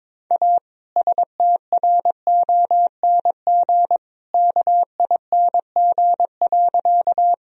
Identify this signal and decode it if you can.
Morse code, heard as a keyed tone, about 22 wpm, 700 Hz. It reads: A STRONG KING.
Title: Sentences from Top 1000 Words - Review Part 3 - 22wpm